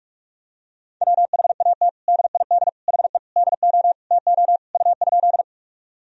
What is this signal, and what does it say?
Morse code, heard as a keyed tone, about 46 wpm, 700 Hz. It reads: WHAT DID HE DO TO U?
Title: Sentences from Top 100 Words and Abbreviations 45wpm